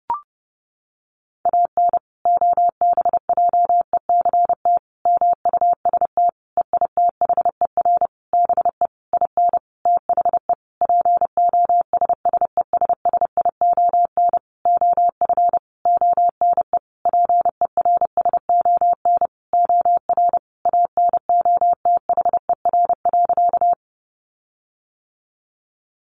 Morse code, heard as a keyed tone, about 30 wpm, 700 Hz. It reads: AN OBJECT MUST EITHER BE IN THE POSSESSION OF ONE PERSON OR ANOTHER.
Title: Sentences from Top 1400 Words 30wpm